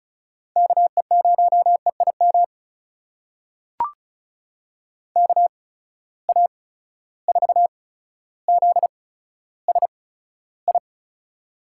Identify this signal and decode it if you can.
Morse code, heard as a keyed tone, about 35 wpm, spaced as for 15 wpm, 700 Hz. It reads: KE0EIM KA4ZSI
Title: USA Call Signs - 6 Characters - 8x Character Spacing - 35wpm